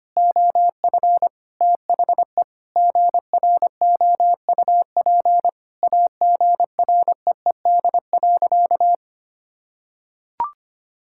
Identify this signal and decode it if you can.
Morse code, heard as a keyed tone, about 25 wpm, 700 Hz. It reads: OF THE GROUP AGREED.
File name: practice / 25wpm / Sentences with 3-Word Phrases - 25wpm